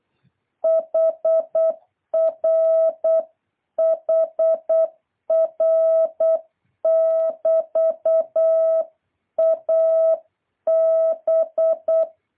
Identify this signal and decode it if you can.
Morse code, heard as a keyed tone, about 8 wpm, 660 Hz. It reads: HR HR <BT> A B
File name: jh8cbh / 2023cw